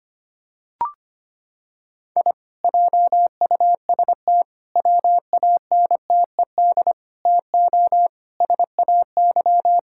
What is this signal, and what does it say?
Morse code, heard as a keyed tone, about 25 wpm, 700 Hz. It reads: I JUST WANTED TO SAY
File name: practice / 25wpm / Sentences with 4-Word Phrases - 25wpm